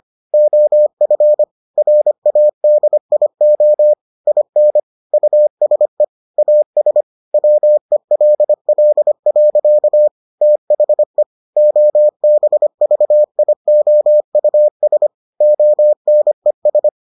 Morse code, heard as a keyed tone, about 25 wpm, 600 Hz. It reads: OF RADIO IN USE AS WELL. THE OBVIOUS ONES